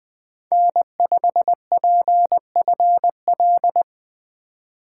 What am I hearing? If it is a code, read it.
Morse code, heard as a keyed tone, about 23 wpm, 700 Hz. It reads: N5PFL
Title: Speed Racing - USA 1x3 Call Signs 20wpm